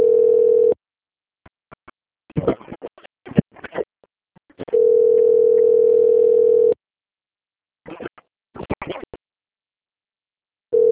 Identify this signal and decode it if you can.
Ringback tone (North America), 440 + 480 Hz.